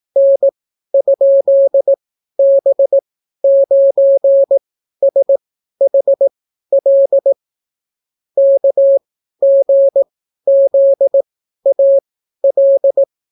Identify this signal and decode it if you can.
Morse code, heard as a keyed tone, about 18 wpm, spaced as for 12 wpm, 550 Hz. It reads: N?B9SHL KGZAL